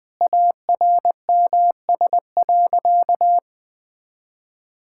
Morse code, heard as a keyed tone, about 20 wpm, 700 Hz. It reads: ARMS.
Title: Sentences with 8-Word Phrases - Spoken Beginning - 20wpm